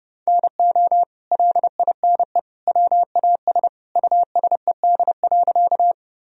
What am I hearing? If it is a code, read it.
Morse code, heard as a keyed tone, about 30 wpm, 700 Hz. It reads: NO LINE WAS USED.
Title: Sentences from Top 1000 Words - Review Part 1 - 30wpm